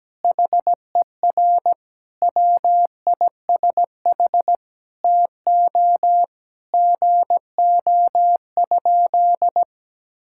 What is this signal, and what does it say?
Morse code, heard as a keyed tone, about 17 wpm, 700 Hz. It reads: HER WISH TO GO?
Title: Sentences from Top 1600 Words 17wpm